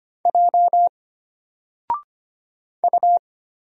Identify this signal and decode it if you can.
Morse code, heard as a keyed tone, about 25 wpm, 700 Hz. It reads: J U
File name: Single-Letter-Number-25wpm